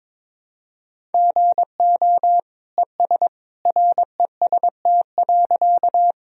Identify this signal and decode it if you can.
Morse code, heard as a keyed tone, about 21 wpm, 700 Hz. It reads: GO ES REST.
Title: Sentences from Top 500 Words and Abbreviations 22wpm